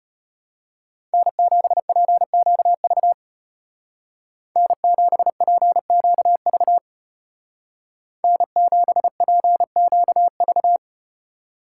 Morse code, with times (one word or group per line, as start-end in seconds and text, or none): 1.1-3.1 N7PQV
4.6-6.8 N7PQV
8.2-10.8 N7PQV